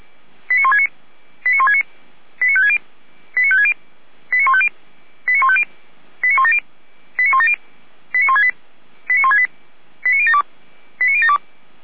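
Five-tone selective call (ZVEI), heard as quick runs of five tones, tones of about 70 ms. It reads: 89169 89169 89560 89560 89150 89150 89170 89170 89168 89168 89072 89072
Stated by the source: ZVEI five-tone selective call received by SDR, archived by Feuerwehr Orlamünde (fire brigade).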